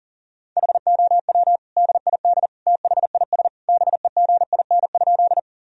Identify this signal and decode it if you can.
Morse code, heard as a keyed tone, about 37 wpm, 700 Hz. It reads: HOW DID THIS BEGIN?
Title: Sentences from Top 1000 Words - Review Part 1 - 40wpm